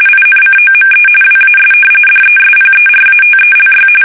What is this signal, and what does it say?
RTTY, about 45.45 baud. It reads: INCOME WAS DESPERATELY
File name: fsk850 / usnavy1